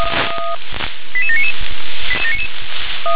Five-tone selective call (ZVEI), heard as quick runs of five tones, tones of about 70 ms. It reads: 88799
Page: SDR/ZVEI